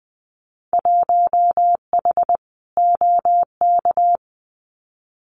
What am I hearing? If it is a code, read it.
Morse code, heard as a keyed tone, about 20 wpm, 700 Hz. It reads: 1H OK
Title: Winter Field Day Exchanges 20wpm